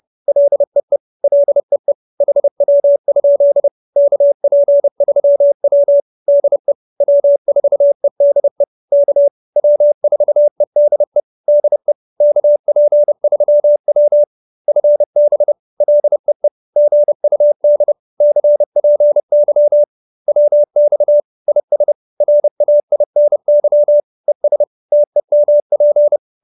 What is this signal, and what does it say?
Morse code, heard as a keyed tone, about 30 wpm, 590 Hz. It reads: LEE LEE HW? KP3W DE W4EDE K W4EDE DE KP3W FB LEE GUD CPY WX IS RAINY ES TEMP